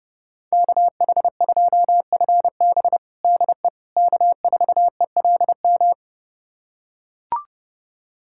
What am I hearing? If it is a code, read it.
Morse code, heard as a keyed tone, about 31 wpm, 700 Hz. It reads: KH2FB DE K4ELM
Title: USA Callsign DE Callsign - 5 Characters - 30wpm